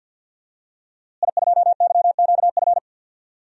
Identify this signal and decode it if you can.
Morse code, heard as a keyed tone, about 51 wpm, 700 Hz. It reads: I2YQF
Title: DXCC-100 Call Signs - 5 Characters - Repeated 3x - 50wpm